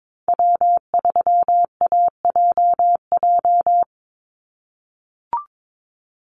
Morse code, heard as a keyed tone, about 22 wpm, 700 Hz. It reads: W3AJJ
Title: USA Call Signs 22wpm